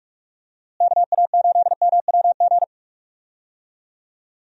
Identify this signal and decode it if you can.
Morse code, heard as a keyed tone, about 46 wpm, 700 Hz. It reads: KA8MWG
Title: USA Call Signs - 6 Characters - Repeated 2x - 45wpm